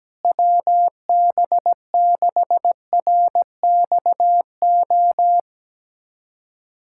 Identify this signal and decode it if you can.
Morse code, heard as a keyed tone, about 17 wpm, 700 Hz. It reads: WB6RXO
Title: USA Callsign DE Callsign - 6 Characters - 17wpm